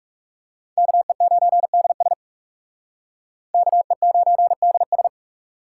Morse code, heard as a keyed tone, about 44 wpm, 700 Hz. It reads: KE9DS KE9DS